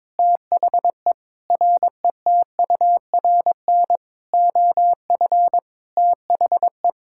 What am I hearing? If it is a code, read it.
Morse code, heard as a keyed tone, about 22 wpm, 700 Hz. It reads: THE RETURN OF THE
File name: Sets of 9 Words - 22wpm